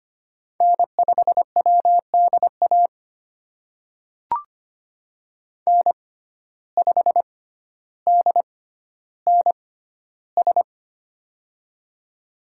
Morse code, heard as a keyed tone, about 25 wpm, spaced as for 13 wpm, 700 Hz. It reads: N5WDA N5DNS
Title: USA Call Signs - 5 Characters - 6x Character Spacing - 25wpm